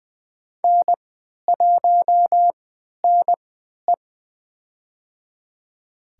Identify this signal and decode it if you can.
Morse code, heard as a keyed tone, about 21 wpm, spaced as for 12 wpm, 700 Hz. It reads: N1NE
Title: USA Call Signs - 4 Characters - 3x Character Spacing - 20wpm